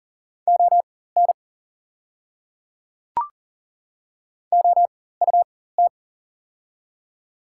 Morse code, heard as a keyed tone, about 39 wpm, spaced as for 18 wpm, 690 Hz. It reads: ON OUT
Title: 4x Character Spacing - Top 100 Words - 40wpm